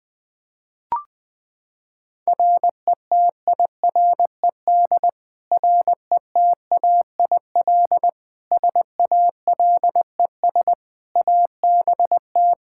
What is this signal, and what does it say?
Morse code, heard as a keyed tone, about 20 wpm, 700 Hz. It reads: RETIRED RETAIL SALES ABT
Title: Practice QSO - Exchange 3 - More Info - 20wpm